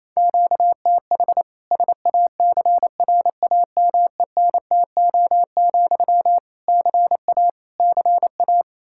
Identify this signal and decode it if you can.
Morse code, heard as a keyed tone, about 28 wpm, 700 Hz. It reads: QTH SACRAMENTO, CA CA
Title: K1USN SST QSO Practice 28wpm